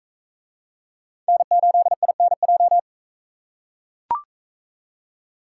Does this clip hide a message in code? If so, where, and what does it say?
Yes: Morse code from 1.3 to 2.8 s, heard as a keyed tone, about 40 wpm, 700 Hz. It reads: N8INJ